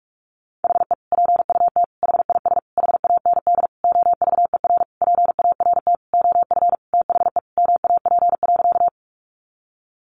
Morse code, heard as a keyed tone, about 44 wpm, 700 Hz. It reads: HE PUT HIS HAND OVER PART OF THE MAP.
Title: Sentences from Top 400 Words 45wpm